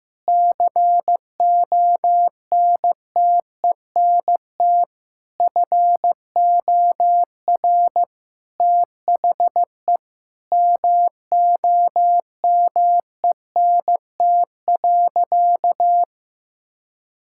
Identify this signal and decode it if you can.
Morse code, heard as a keyed tone, about 16 wpm, 700 Hz. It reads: CONTENT FOR THE MOMENT.
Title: Sentences from Top 1000 Words - Review Part 6 - 15wpm